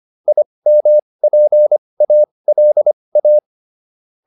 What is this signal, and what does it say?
Morse code, heard as a keyed tone, about 25 wpm, spaced as for 20 wpm, 600 Hz. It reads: IMPALA